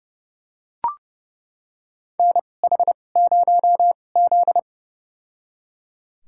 Morse code, heard as a keyed tone, about 30 wpm, spaced as for 22 wpm, 700 Hz. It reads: NH0Z
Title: USA Call Signs - 4 Characters - 2x Character Spacing - 30wpm